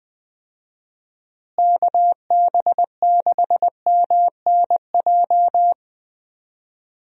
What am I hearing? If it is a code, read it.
Morse code, heard as a keyed tone, about 19 wpm, 700 Hz. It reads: KB6MNJ
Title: USA Call Signs - 6 Characters - Repeated 2x - 20wpm